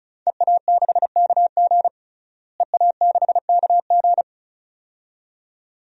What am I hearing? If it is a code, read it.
Morse code, heard as a keyed tone, about 36 wpm, 700 Hz. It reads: EA6KG EA6KG
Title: DXCC-100 Call Signs - 5 Characters - Repeated 2x - 35wpm